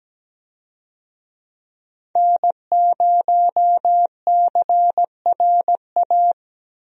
Morse code, heard as a keyed tone, about 17 wpm, 700 Hz. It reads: N0CRA
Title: USA Call Signs - 5 Characters - 17wpm